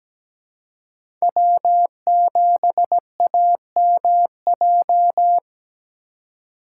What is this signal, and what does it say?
Morse code, heard as a keyed tone, about 17 wpm, 700 Hz. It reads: W7AMJ
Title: USA Call Signs - 5 Characters - 4x Character Spacing - 17wpm